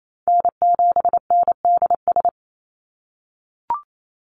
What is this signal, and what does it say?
Morse code, heard as a keyed tone, about 28 wpm, 700 Hz. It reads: N7NDS
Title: USA Call Signs 28wpm